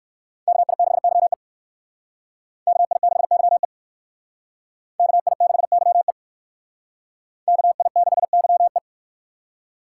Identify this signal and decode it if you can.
Morse code, heard as a keyed tone, about 58 wpm, 700 Hz. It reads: KI6YE KI6YE KI6YE KI6YE